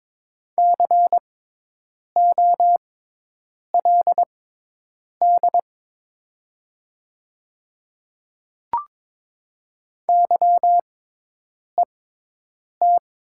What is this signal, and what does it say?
Morse code, heard as a keyed tone, about 22 wpm, spaced as for 8 wpm, 700 Hz. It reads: COLD YET
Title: 6x Character Spacing - Top 500 Words - 22wpm